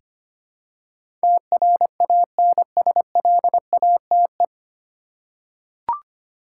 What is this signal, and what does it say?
Morse code, heard as a keyed tone, about 25 wpm, 700 Hz. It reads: TRANSLATE
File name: Nine Letter Words - 25wpm